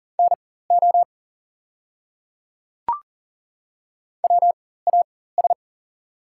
Morse code, heard as a keyed tone, about 40 wpm, spaced as for 19 wpm, 690 Hz. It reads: NO WAS
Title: 4x Character Spacing - Top 100 Words - 40wpm